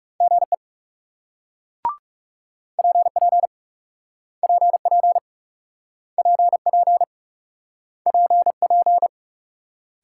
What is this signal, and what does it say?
Morse code, heard as a keyed tone, about 40 wpm, 700 Hz. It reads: GE PP PP PP PP